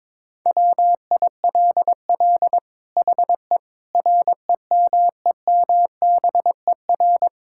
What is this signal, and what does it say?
Morse code, heard as a keyed tone, about 22 wpm, 700 Hz. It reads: WILL HE REMEMBER
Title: Sentences from Top 1000 Words - Review Part 6 - 22wpm